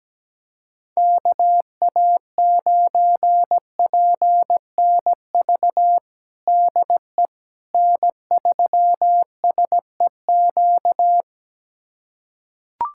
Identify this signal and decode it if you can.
Morse code, heard as a keyed tone, about 17 wpm, 700 Hz. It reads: KA9PNV DE N3SEQ
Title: USA Callsign DE Callsign 17wpm